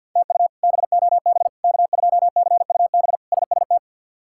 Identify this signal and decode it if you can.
Morse code, heard as a keyed tone, about 51 wpm, 700 Hz. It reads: TU BOB K2CUB SST
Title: K1USN SST QSO Practice - 50wpm